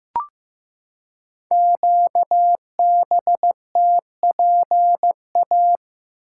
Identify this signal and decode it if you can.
Morse code, heard as a keyed tone, about 15 wpm, 700 Hz. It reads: QBTPA
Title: Code Groups - 5 Letters - 15wpm